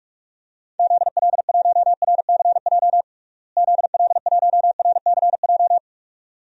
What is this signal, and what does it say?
Morse code, heard as a keyed tone, about 46 wpm, 700 Hz. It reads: ZL1RCJ ZL1RCJ
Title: DXCC-100 Call Signs - 6 Characters - Repeated 2x - 45wpm